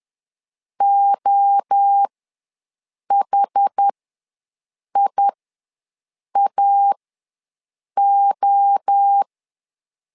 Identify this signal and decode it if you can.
Morse code, heard as a keyed tone, about 10 wpm, spaced as for 6 wpm, 800 Hz. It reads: OHIAO
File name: LEZIONI / 3 MWRO / 3 MWRO ws 10wpm 2310